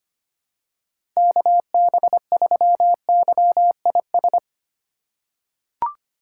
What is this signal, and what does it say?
Morse code, heard as a keyed tone, about 25 wpm, 700 Hz. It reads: KB3YIS
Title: USA Call Signs - 8x Character Spacing - 25wpm